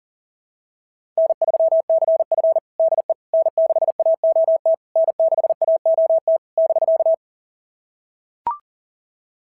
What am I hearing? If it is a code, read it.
Morse code, heard as a keyed tone, about 40 wpm, 660 Hz. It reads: N3CF DE N6AOT N6AOT <BK>
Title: SOTA QSO Practice 40wpm